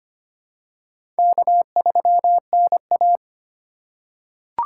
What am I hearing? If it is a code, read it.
Morse code, heard as a keyed tone, about 25 wpm, 700 Hz. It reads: K3NA
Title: USA Call Signs - 4 Characters - 3x Character Spacing - 25wpm